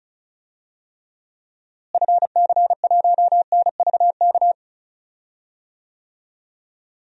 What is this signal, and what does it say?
Morse code, heard as a keyed tone, about 34 wpm, 700 Hz. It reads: FC1NVK
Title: DXCC-100 Call Signs - 6 Characters - Repeated 3x - 35wpm